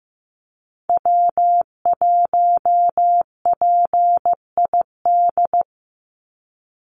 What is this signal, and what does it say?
Morse code, heard as a keyed tone, about 15 wpm, 700 Hz. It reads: W1PID
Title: Call Signs Repeated 3x - 15wpm